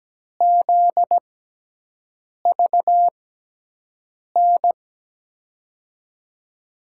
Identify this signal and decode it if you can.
Morse code, heard as a keyed tone, about 17 wpm, spaced as for 6 wpm, 700 Hz. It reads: ZVN
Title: USA Call Signs - 6x Character Spacing - 17wpm